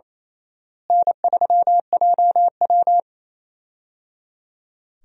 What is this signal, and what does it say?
Morse code, heard as a keyed tone, about 28 wpm, 700 Hz. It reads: N3JW
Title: USA Call Signs - 4 Characters - Repeated 3x - 28wpm